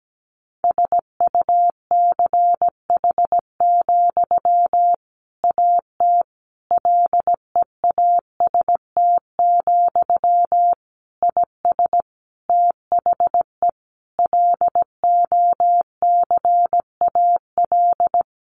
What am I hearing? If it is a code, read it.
Morse code, heard as a keyed tone, about 17 wpm, 700 Hz. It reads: SUCH, AT LEAST, IS THE LOCAL
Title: Sentences from Top 1500 Words 17wpm